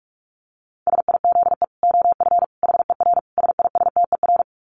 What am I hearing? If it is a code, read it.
Morse code, heard as a keyed tone, about 44 wpm, 700 Hz. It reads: SIZE OF HER SISTER
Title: Dit Challenge 45wpm